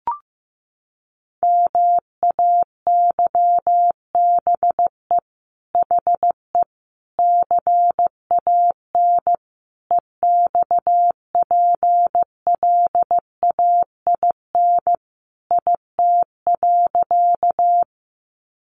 Morse code, heard as a keyed tone, about 15 wpm, 700 Hz. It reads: MAYBE HE CAN EXPLAIN IT.
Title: Sentences from Top 1100 Words 15wpm